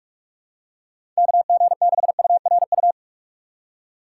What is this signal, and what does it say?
Morse code, heard as a keyed tone, about 48 wpm, 700 Hz. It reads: KG6URU
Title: USA Call Signs - 6 Characters - 6x Character Spacing - 45wpm